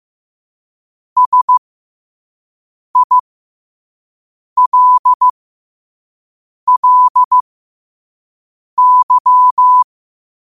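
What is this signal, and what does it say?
Morse code, heard as a keyed tone, about 15 wpm, spaced as for 5 wpm, 1000 Hz. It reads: SILLY
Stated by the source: Morse code quote of the day sent at 5 WPM.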